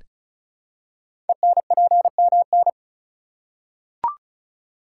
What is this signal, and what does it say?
Morse code, heard as a keyed tone, about 34 wpm, 700 Hz. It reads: ENPMN
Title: Code Groups - 5 Letter:Numbers - 35wpm